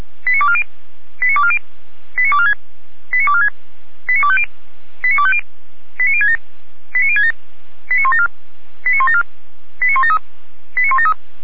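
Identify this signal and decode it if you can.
Five-tone selective call (ZVEI), heard as quick runs of five tones, tones of about 70 ms. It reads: 89240 89240 89246 89246 89250 89250 89067 89067 89174 89174 89173 89173
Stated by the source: ZVEI five-tone call received by SDR, logged on audio.f85360.feuerwehr-orlamuende.org.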